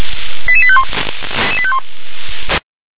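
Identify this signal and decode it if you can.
Five-tone selective call (ZVEI), heard as quick runs of five tones, tones of about 70 ms. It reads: 88851
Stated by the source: ZVEI five-tone selective call 88852, received by SDR.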